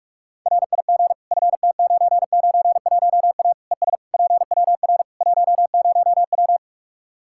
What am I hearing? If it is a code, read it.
Morse code, heard as a keyed tone, about 46 wpm, 700 Hz. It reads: RIG FT991A ES PWR 10W
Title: Practice QSO - Exchange 2 - Info - 45wpm